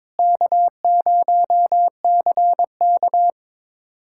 Morse code, heard as a keyed tone, about 22 wpm, 700 Hz. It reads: K0CK
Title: USA Callsign DE Callsign - 4 Characters - 22wpm